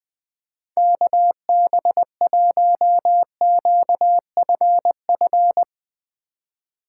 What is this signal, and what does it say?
Morse code, heard as a keyed tone, about 20 wpm, 700 Hz. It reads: KB1QFF
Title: USA Call Signs - 6 Characters - 3x Character Spacing - 20wpm